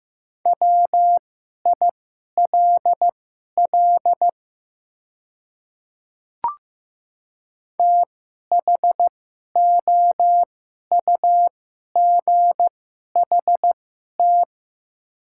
Morse code, heard as a keyed tone, about 15 wpm, spaced as for 11 wpm, 700 Hz. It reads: WILL THOUGHT
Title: 2x Character Spacing - Top 300 Words - 15wpm